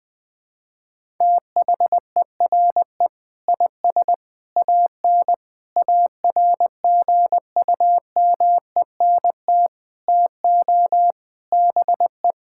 Morse code, heard as a keyed tone, about 20 wpm, 700 Hz. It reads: THERE IS AN ARGUMENT TO BE
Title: Sentences with 8-Word Phrases - 20wpm